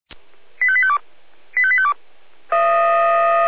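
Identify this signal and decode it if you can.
Five-tone selective call (ZVEI), heard as quick runs of five tones, tones of about 70 ms. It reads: 85852 85852
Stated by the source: ZVEI five-tone selective call 85810 received by SDR, from the Feuerwehr Orlamünde (fire brigade) audio server.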